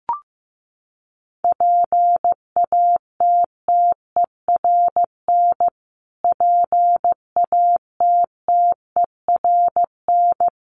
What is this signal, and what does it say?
Morse code, heard as a keyed tone, about 15 wpm, 700 Hz. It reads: PATTERN PATTERN
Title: Top 500 Words Repeated 15wpm